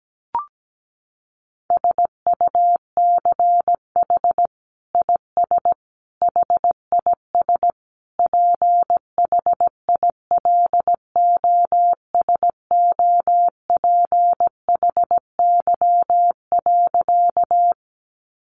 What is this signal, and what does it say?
Morse code, heard as a keyed tone, about 17 wpm, 700 Hz. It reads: SUCH IS HIS PHILOSOPHY.
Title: Sentences from Top 900 Words 17wpm